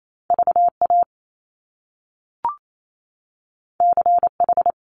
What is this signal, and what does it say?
Morse code, heard as a keyed tone, about 28 wpm, 700 Hz. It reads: VA CH
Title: Top Two Letter Combinations in Words 28wpm